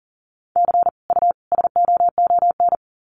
Morse code, heard as a keyed tone, about 40 wpm, 700 Hz. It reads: C U SOON
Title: Common QSO Phrases 40wpm